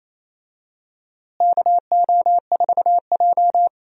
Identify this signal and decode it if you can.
Morse code, heard as a keyed tone, about 28 wpm, 700 Hz. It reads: KO4J
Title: USA Call Signs - 4 Characters - Repeated 2x - 28wpm